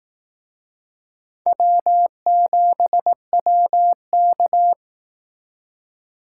Morse code, heard as a keyed tone, about 18 wpm, 700 Hz. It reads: W7WK
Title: Speed Racing - USA 1x2 Call Signs 15wpm